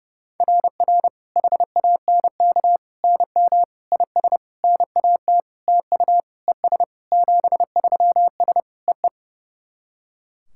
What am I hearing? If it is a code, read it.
Morse code, heard as a keyed tone, about 30 wpm, 700 Hz. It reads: RR HANK NM IS NAT TU ES 73S EE
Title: SOTA QSO Practice 30wpm